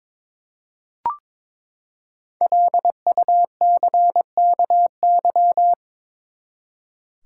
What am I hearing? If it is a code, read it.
Morse code, heard as a keyed tone, about 22 wpm, 700 Hz. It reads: LUCKY
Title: Five Letter Words - 22wpm